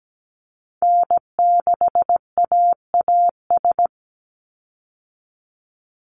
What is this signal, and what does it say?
Morse code, heard as a keyed tone, about 17 wpm, 700 Hz. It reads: N6AAS
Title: USA Call Signs 17wpm